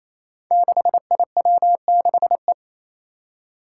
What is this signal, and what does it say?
Morse code, heard as a keyed tone, about 28 wpm, 700 Hz. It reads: 6IW6E
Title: Five Challenging Letter-Number Combinations 28wpm